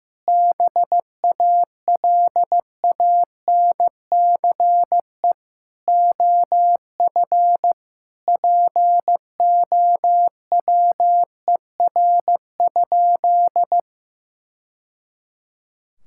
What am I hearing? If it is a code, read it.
Morse code, heard as a keyed tone, about 15 wpm, 700 Hz. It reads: BALANCE OF POWER?
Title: Sentences from Top 1000 Words - Review Part 6 - 15wpm